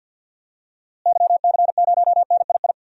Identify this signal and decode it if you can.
Morse code, heard as a keyed tone, about 48 wpm, 700 Hz. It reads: YC0NII
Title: DXCC-100 Call Signs - 6 Characters - Repeated 2x - 50wpm